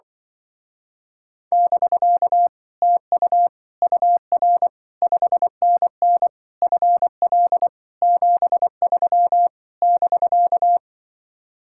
Morse code, heard as a keyed tone, about 24 wpm, 700 Hz. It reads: <BK> TU UR 5NN FL 73 <BK>
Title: Speed-Racing Common QSO Phrases Encore - 20wpm